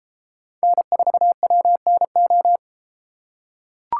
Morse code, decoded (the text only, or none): N4WNO